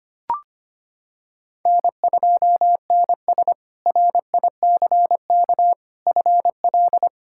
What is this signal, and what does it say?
Morse code, heard as a keyed tone, about 25 wpm, 700 Hz. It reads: N2NS RICK FL
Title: K1USN SST QSO Practice - 25wpm